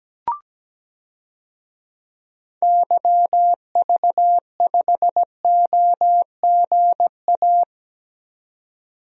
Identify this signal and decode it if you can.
Morse code, heard as a keyed tone, about 17 wpm, 700 Hz. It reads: YV5OGA
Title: DXCC-100 Call Signs - 6 Characters - 17wpm